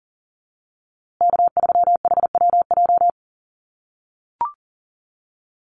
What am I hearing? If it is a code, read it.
Morse code, heard as a keyed tone, about 40 wpm, 700 Hz. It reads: K3HWJ
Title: USA Call Signs Repeated 3x 40wpm